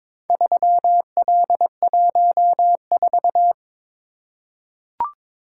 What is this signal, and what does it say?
Morse code, heard as a keyed tone, about 22 wpm, 700 Hz. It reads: 3L14
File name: practice / 22wpm / Four Letter-Number Combinations - 22wpm